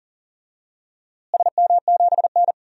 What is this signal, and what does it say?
Morse code, heard as a keyed tone, about 40 wpm, 700 Hz. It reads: SM7N
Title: DXCC-100 Call Signs - 4 Characters - Repeated 2x - 40wpm